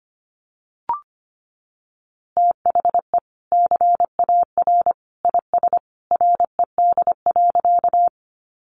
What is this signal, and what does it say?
Morse code, heard as a keyed tone, about 25 wpm, 700 Hz. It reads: THE CAR IS RED.
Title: Easy Phrases 25wpm